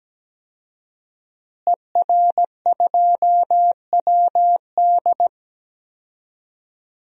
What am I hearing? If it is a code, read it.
Morse code, heard as a keyed tone, about 17 wpm, 700 Hz. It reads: ER2WD
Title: DXCC-100 Call Signs - 5 Characters - Repeated 3x - 17wpm